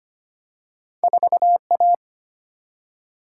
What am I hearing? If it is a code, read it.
Morse code, heard as a keyed tone, about 25 wpm, 700 Hz. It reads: 4A
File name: Two Letter-Number Combinations - 25wpm